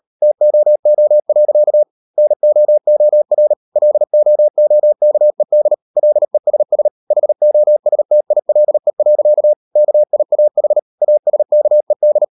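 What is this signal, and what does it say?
Morse code, heard as a keyed tone, about 37 wpm, 600 Hz. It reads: TOO. NOOR LOOKED LESS HOSTILE. KIAH ASKED
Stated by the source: Morse code practice at 38 WPM, sending Ken Follett's novel Never in English, from CW med Gnister.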